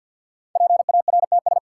Morse code, heard as a keyed tone, about 46 wpm, 700 Hz. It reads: PARTS
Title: Sentences from Top 1000 Words - Review Part 6 - New Word Drill - 50wpm